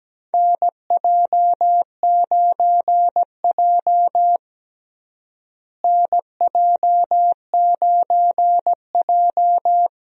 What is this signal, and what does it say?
Morse code, heard as a keyed tone, about 17 wpm, 700 Hz. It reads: NJ9J NJ9J
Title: USA Call Signs - 4 Characters - Repeated 2x - 17wpm